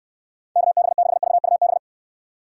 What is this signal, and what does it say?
Morse code, heard as a keyed tone, about 69 wpm, 700 Hz. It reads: KB6VUB